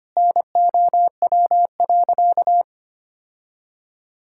Morse code, heard as a keyed tone, about 25 wpm, 700 Hz. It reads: NOW.
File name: practice / 25wpm / Sentences with 3-Word Phrases - 25wpm